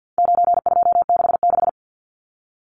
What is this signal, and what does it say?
Morse code, heard as a keyed tone, about 50 wpm, 700 Hz. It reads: 9266
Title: Numbers - 4-Digit 50wpm